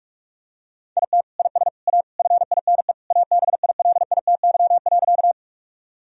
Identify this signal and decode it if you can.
Morse code, heard as a keyed tone, about 46 wpm, 700 Hz. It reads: IT IS A FINE ABILITY.
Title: Sentences from Top 1000 Words - Review Part 3 - 45wpm